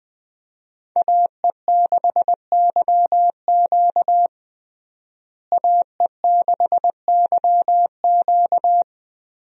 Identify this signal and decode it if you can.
Morse code, heard as a keyed tone, about 19 wpm, 700 Hz. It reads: AE6YQ AE6YQ
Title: USA Call Signs - 5 Characters - Repeated 3x - 20wpm